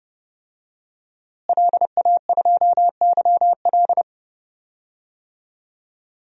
Morse code, heard as a keyed tone, about 31 wpm, 700 Hz. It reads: LA2YL
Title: DXCC-100 Call Signs - 5 Characters - Repeated 2x - 30wpm